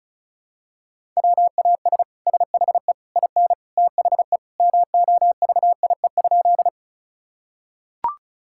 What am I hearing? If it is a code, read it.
Morse code, heard as a keyed tone, about 34 wpm, 700 Hz. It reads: WAS SHE IN THE MOVIE?
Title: Sentences from Top 1000 Words - Review Part 5 - 35wpm